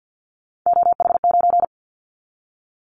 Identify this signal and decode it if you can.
Morse code, heard as a keyed tone, about 50 wpm, 700 Hz. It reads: OH9
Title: Callsign Prefixes for Top 25 DXCC - 50wpm